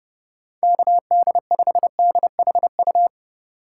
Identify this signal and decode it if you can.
Morse code, heard as a keyed tone, about 31 wpm, 700 Hz. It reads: KD5DHU